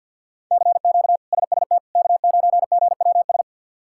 Morse code, heard as a keyed tone, about 52 wpm, 700 Hz. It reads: CQ SST K9GWS
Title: K1USN SST QSO Practice - 50wpm